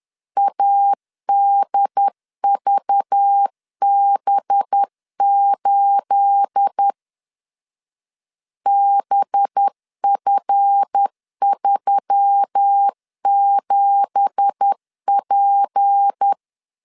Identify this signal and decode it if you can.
Morse code, heard as a keyed tone, about 10 wpm, 800 Hz. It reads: ADVB8 BF37P
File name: CW / DET / 1 9 1500C S B 10wpm 4439